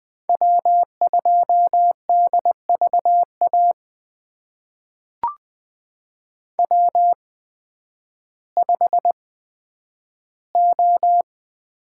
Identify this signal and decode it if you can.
Morse code, heard as a keyed tone, about 20 wpm, spaced as for 11 wpm, 700 Hz. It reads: W2DVA W5O
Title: USA Call Signs - 5 Characters - 8x Character Spacing - 20wpm